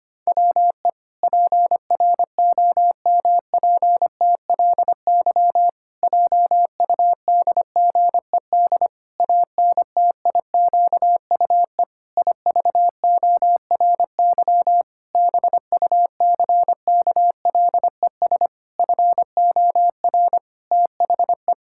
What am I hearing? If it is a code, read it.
Morse code, heard as a keyed tone, about 25 wpm, 700 Hz. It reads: WE PROMPTLY JUDGED ANTIQUE IVORY BUCKLES FOR THE
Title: Pangram Challenge v2 - 25wpm